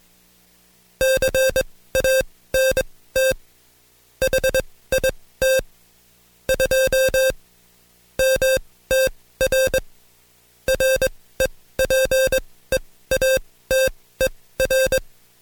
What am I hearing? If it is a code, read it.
Morse code, heard as a keyed tone, about 22 wpm, spaced as for 16 wpm, 530 Hz. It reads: CANT HIT 2 MTR REPEATER
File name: Code Quick / AudioDownld2 / Seg134-Stepper7 15WPM